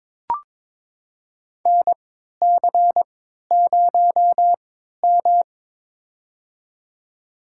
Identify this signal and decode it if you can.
Morse code, heard as a keyed tone, about 22 wpm, spaced as for 13 wpm, 700 Hz. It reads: NC0M
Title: USA Call Signs - 4 Characters - 3x Character Spacing - 22wpm